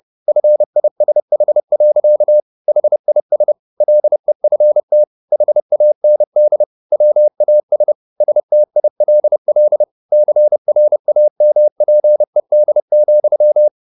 Morse code, heard as a keyed tone, about 30 wpm, 600 Hz. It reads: FISH. HIS LEFT HAND WAS STILL CRAMPED,
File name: the old man and the sea / the old man and the sea 1 30wpm